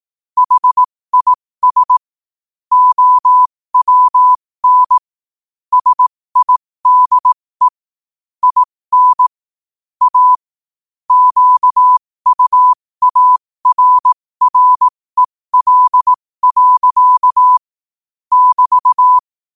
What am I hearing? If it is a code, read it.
Morse code, heard as a keyed tone, about 18 wpm, spaced as for 15 wpm, 1000 Hz. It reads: HIS OWN SIDE IN A QUARREL. <BT>
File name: qotdfiles / 15wpm